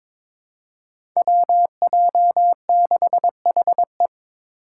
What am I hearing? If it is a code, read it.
Morse code, heard as a keyed tone, about 22 wpm, 700 Hz. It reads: WJ6HE
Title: Five Challenging Letter-Number Combinations 22wpm